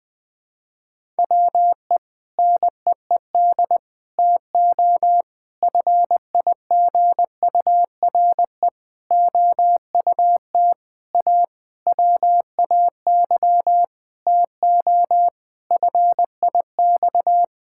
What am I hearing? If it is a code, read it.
Morse code, heard as a keyed tone, about 20 wpm, 700 Hz. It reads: WE NEED TO FIGURE OUT A WAY TO FIX
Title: Sentences with 6-Word Phrases - 20wpm